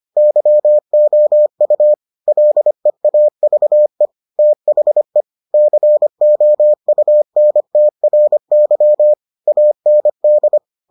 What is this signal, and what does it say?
Morse code, heard as a keyed tone, about 25 wpm, 600 Hz. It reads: YOU LEAVE THE COUNTRY AND